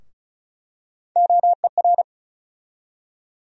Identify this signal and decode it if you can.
Morse code, heard as a keyed tone, about 36 wpm, 700 Hz. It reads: OER
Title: USA Call Sign Postfixes - 3 Letters 35wpm